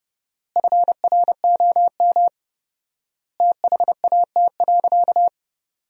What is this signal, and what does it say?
Morse code, heard as a keyed tone, about 30 wpm, 700 Hz. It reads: FROM THAT.
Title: 4x Word-Spacing - Sentences from Top 300 Words - 30wpm